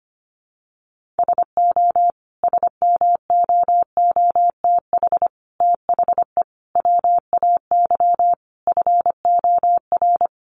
Morse code, heard as a keyed tone, about 25 wpm, 700 Hz. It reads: SO SMOOTH THE WAY FOR
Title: Sentences from Top 1400 Words 25wpm